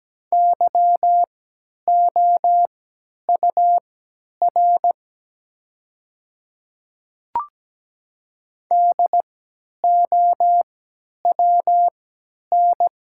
Morse code, heard as a keyed tone, about 17 wpm, spaced as for 10 wpm, 690 Hz. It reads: YOUR DOWN
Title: 3x Character Spacing - Top 100 Words - 17wpm